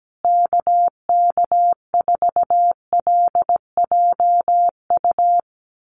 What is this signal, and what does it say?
Morse code, heard as a keyed tone, about 17 wpm, 700 Hz. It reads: KK4LJU
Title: USA Call Signs Repeated 2x 17wpm